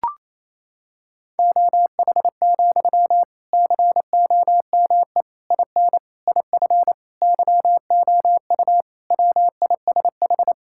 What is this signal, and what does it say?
Morse code, heard as a keyed tone, about 28 wpm, 700 Hz. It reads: OH, COME IN IF YOU WISH
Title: Sentences from Top 1600 Words 28wpm